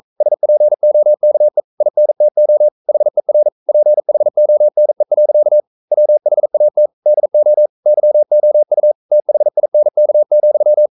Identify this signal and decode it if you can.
Morse code, heard as a keyed tone, about 42 wpm, 600 Hz. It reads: SPOKE INTO HER PHONE. WHAT DO YOU THINK,
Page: Romanen Never af Ken Follet på engelsk.